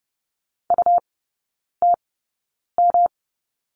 Morse code, heard as a keyed tone, about 30 wpm, spaced as for 9 wpm, 700 Hz. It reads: UTM
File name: Lesson 18e - Letter m - Three Characters 30wpm